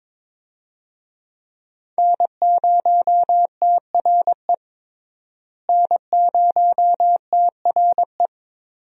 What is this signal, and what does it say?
Morse code, heard as a keyed tone, about 22 wpm, 700 Hz. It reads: N0TRE N0TRE
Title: USA Call Signs - 5 Characters - Repeated 2x - 22wpm